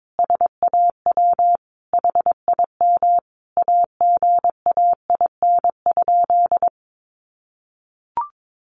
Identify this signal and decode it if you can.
Morse code, heard as a keyed tone, about 22 wpm, 700 Hz. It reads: SAW HIM AGAIN?
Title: Sentences from Top 1400 Words 22wpm